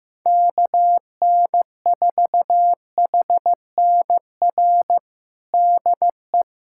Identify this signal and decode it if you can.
Morse code, heard as a keyed tone, about 16 wpm, 700 Hz. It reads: KN4HNR DE
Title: USA Callsign DE Callsign - 6 Characters - 15wpm